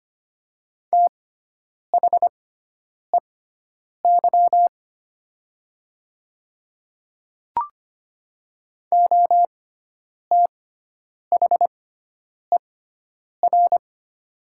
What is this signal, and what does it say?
Morse code, heard as a keyed tone, about 25 wpm, spaced as for 9 wpm, 700 Hz. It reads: THEY OTHER
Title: 6x Character Spacing - Top 200 Words - 25wpm